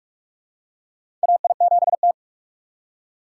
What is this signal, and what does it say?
Morse code, heard as a keyed tone, about 41 wpm, 700 Hz. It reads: AI7T
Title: USA Call Signs - 2x Character Spacing - 45wpm